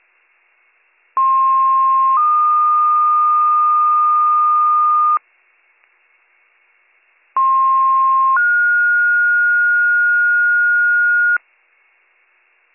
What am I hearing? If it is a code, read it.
Two-tone page, heard as a one-second tone, then a three-second tone, about 1060 then 1220 Hz, about 1060 then 1430 Hz. It is heard from 1.2 to 11.4 s.